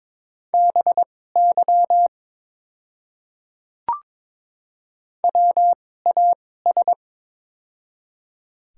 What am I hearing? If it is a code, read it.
Morse code, heard as a keyed tone, about 22 wpm, spaced as for 16 wpm, 700 Hz. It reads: BY WAS
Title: 2x Character Spacing - Top 100 Words - 22wpm